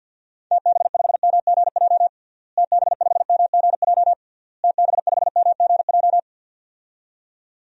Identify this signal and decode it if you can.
Morse code, heard as a keyed tone, about 51 wpm, 700 Hz. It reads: TB5MGJ TB5MGJ TB5MGJ
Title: DXCC-100 Call Signs - 6 Characters - Repeated 3x - 50wpm